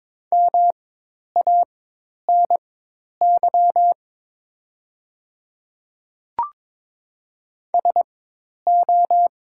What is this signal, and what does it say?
Morse code, heard as a keyed tone, about 22 wpm, spaced as for 10 wpm, 700 Hz. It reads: MANY SO
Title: 4x Character Spacing - Top 100 Words - 22wpm